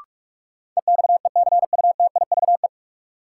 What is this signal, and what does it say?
Morse code, heard as a keyed tone, about 41 wpm, 700 Hz. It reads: EXECUTIVE
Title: Sets of 5 Words Encore - New Word Drill - 45wpm